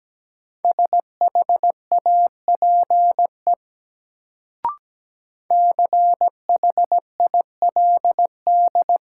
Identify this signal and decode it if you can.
Morse code, heard as a keyed tone, about 17 wpm, 700 Hz. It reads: SHAPE CHILD
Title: Sets of 5 Words Encore - New Word Drill - 17wpm